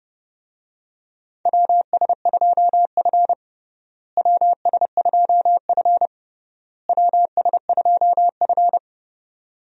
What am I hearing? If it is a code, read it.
Morse code, heard as a keyed tone, about 32 wpm, 700 Hz. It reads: WS2F WS2F WS2F
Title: USA Call Signs - 4 Characters - Repeated 3x - 30wpm